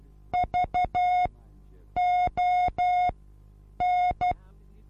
Morse code, heard as a keyed tone, about 12 wpm, spaced as for 8 wpm, 720 Hz. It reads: VON
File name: Onlinelehrgang / morsen / DJ4UF Morsen23